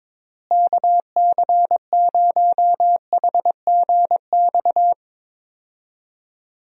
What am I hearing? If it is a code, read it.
Morse code, heard as a keyed tone, about 22 wpm, 700 Hz. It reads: KC0HGX
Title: USA Call Signs - 6 Characters - Repeated 3x - 22wpm